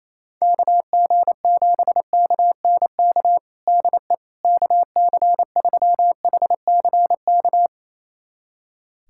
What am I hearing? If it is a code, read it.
Morse code, heard as a keyed tone, about 28 wpm, 700 Hz. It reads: KG7KNK DE KC3HCK
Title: USA Callsign DE Callsign - 6 Characters - 28wpm